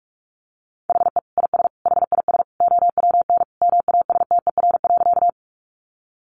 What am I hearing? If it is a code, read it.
Morse code, heard as a keyed tone, about 44 wpm, 700 Hz. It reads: HE IS HIS OWN MASTER.
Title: Sentences from Top 1300 Words 45wpm